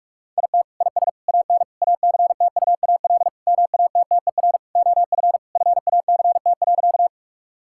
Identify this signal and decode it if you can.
Morse code, heard as a keyed tone, about 41 wpm, 700 Hz. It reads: IT IS AN ACTUAL MATTER OF FACT.
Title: Sentences from Top 1000 Words - Review Part 5 - 45wpm